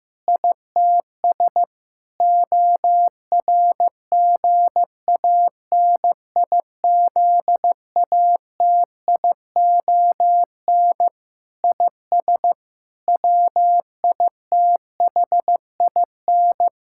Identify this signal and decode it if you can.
Morse code, heard as a keyed tone, about 15 wpm, 700 Hz. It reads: ITS ORGANIZATION IS WITHIN
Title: Sentences from Top 1000 Words - Review Part 5 - 15wpm